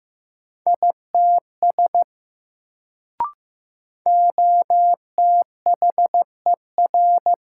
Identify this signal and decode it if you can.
Morse code, heard as a keyed tone, about 15 wpm, 700 Hz. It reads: ITS OTHER
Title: Sets of 2 Words - New Word Drill - 15wpm